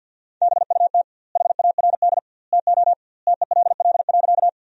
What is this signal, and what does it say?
Morse code, heard as a keyed tone, about 53 wpm, 700 Hz. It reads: BUT HARD TO TELL.
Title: 2x Word-Spacing - Sentences from Top 300 Words - 50wpm